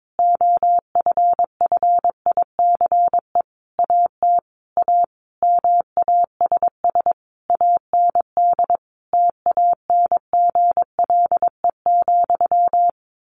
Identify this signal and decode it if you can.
Morse code, heard as a keyed tone, about 22 wpm, 700 Hz. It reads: OFFICE AT A MASS AND TANGLE,
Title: A Sound Of Thunder 22wpm